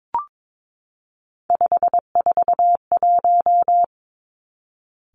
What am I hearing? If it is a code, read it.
Morse code, heard as a keyed tone, about 22 wpm, 700 Hz. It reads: 541
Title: Three Challenging Letter-Number Combinations 22wpm